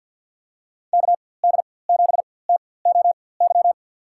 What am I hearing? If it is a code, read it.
Morse code, heard as a keyed tone, about 53 wpm, spaced as for 24 wpm, 700 Hz. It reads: KD7TOY
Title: USA Call Signs - 4x Character Spacing - 50wpm